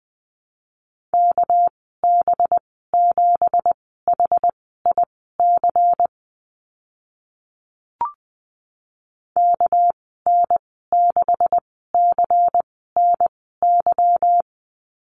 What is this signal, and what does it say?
Morse code, heard as a keyed tone, about 20 wpm, spaced as for 14 wpm, 700 Hz. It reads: KB7HIC KN6CNY
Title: 213 Lesson 34g - Number 8 - Call Signs 20wpm